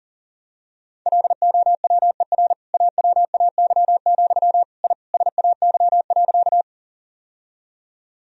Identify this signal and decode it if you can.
Morse code, heard as a keyed tone, about 40 wpm, 690 Hz. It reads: LOWER AWAY, I SAY.
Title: Sentences from Top 1000 Words - Review Part 4 - 40wpm